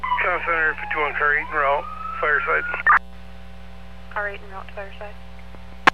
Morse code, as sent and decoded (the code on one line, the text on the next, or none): none